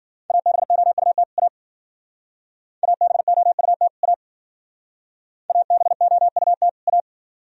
Morse code, heard as a keyed tone, about 52 wpm, 700 Hz. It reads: ABOUT A ABOUT A ABOUT A